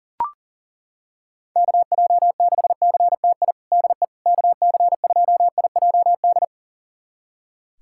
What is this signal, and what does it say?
Morse code, heard as a keyed tone, about 40 wpm, 700 Hz. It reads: KJ6CTI DE KC2IJD
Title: USA Callsign DE Callsign - 6 Characters - 40wpm